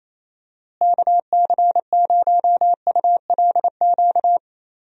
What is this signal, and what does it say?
Morse code, heard as a keyed tone, about 28 wpm, 700 Hz. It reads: KC0ULQ
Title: USA Call Signs - 6 Characters - Repeated 3x - 28wpm